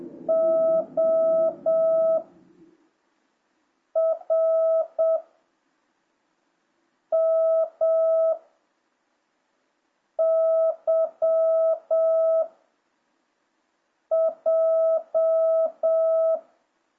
Morse code, heard as a keyed tone, about 7 wpm, 660 Hz. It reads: O R M Y J